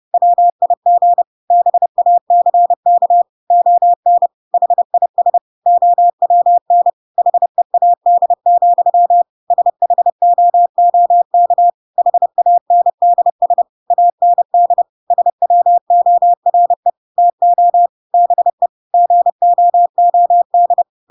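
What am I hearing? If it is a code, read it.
Morse code, heard as a keyed tone, about 30 wpm, 700 Hz. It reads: WIG BACK ON HIS OWN HEAD, SHOOK HANDS AND SWORE TO BE GOOD
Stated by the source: Chapter 2 of The Adventures of Pinocchio read as a book in Morse code (CW) at 30 wpm, by ik1hge.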